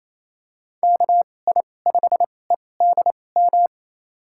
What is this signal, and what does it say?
Morse code, heard as a keyed tone, about 28 wpm, spaced as for 20 wpm, 700 Hz. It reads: KI5EDM
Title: USA Call Signs - 2x Character Spacing - 28wpm